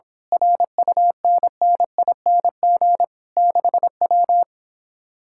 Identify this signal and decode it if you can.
Morse code, heard as a keyed tone, about 25 wpm, 700 Hz. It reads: RUNNING 6W
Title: Speed-Racing Common QSO Phrases - 22wpm